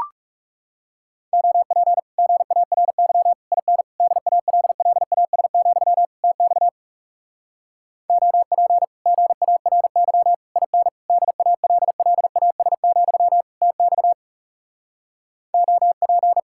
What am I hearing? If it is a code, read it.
Morse code, heard as a keyed tone, about 40 wpm, 700 Hz. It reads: OP GARY IN DALLAS, TX OP GARY IN DALLAS, TX OP